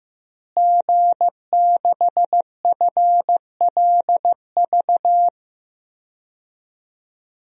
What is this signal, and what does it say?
Morse code, heard as a keyed tone, about 15 wpm, 700 Hz. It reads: G6FLV
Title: DXCC-100 Call Signs - 5 Characters - Repeated 3x - 15wpm